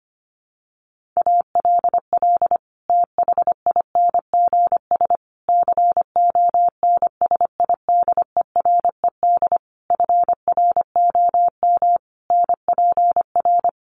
Morse code, heard as a keyed tone, about 25 wpm, 700 Hz. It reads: ALL THINGS CONSIDERED FROM NPR
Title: Sets of 5 Words Encore 25wpm